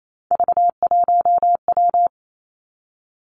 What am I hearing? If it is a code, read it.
Morse code, heard as a keyed tone, about 28 wpm, 690 Hz. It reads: V1W
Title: Three Challenging Letter-Number Combinations 28wpm